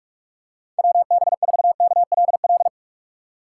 Morse code, heard as a keyed tone, about 45 wpm, 700 Hz. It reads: WB4KLL
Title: USA Call Signs - 6x Character Spacing - 45wpm